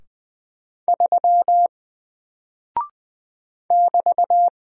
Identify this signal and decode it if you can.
Morse code, heard as a keyed tone, about 20 wpm, 700 Hz. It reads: 3 <BT>
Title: Single Character (Letter:Number:Punctuation:Prosign) Alt 20wpm